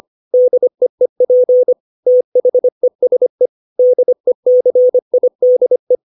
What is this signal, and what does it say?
Morse code, heard as a keyed tone, about 25 wpm, 500 Hz. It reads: DEEP THESE DECIDE
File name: 632 words / 632 Words 25WPM